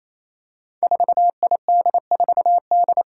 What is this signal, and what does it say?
Morse code, heard as a keyed tone, about 28 wpm, 700 Hz. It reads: 4ID4D
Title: Five Challenging Letter-Number Combinations 28wpm